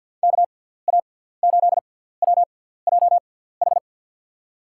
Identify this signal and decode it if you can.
Morse code, heard as a keyed tone, about 50 wpm, spaced as for 17 wpm, 700 Hz. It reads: KA8WJH
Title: USA Call Signs - 6 Characters - 6x Character Spacing - 50wpm